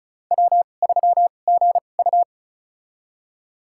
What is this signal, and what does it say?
Morse code, heard as a keyed tone, about 36 wpm, spaced as for 25 wpm, 700 Hz. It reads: W3GU
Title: USA Call Signs - 4 Characters - 2x Character Spacing - 35wpm